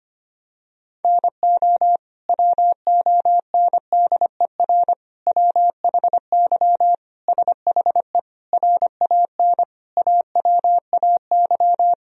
Morse code, heard as a keyed tone, about 25 wpm, 700 Hz. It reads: NO WONDER WHY SHE RAN AWAY
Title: Sentences from Top 1000 Words - Review Part 3 - 25wpm